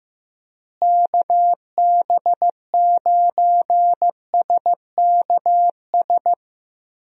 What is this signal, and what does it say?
Morse code, heard as a keyed tone, about 15 wpm, 700 Hz. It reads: KB9SKS
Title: USA Call Signs - 6 Characters - Repeated 2x - 15wpm